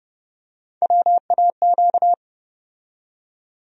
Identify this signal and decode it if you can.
Morse code, heard as a keyed tone, about 31 wpm, 700 Hz. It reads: WAQ
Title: USA Call Sign Postfixes - 3 Letters 30wpm